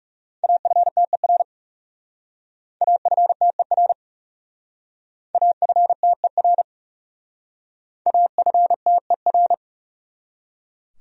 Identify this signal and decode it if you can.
Morse code, heard as a keyed tone, about 36 wpm, 700 Hz. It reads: AFTER AFTER AFTER AFTER
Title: Speed-Racing Top 300 Words 30wpm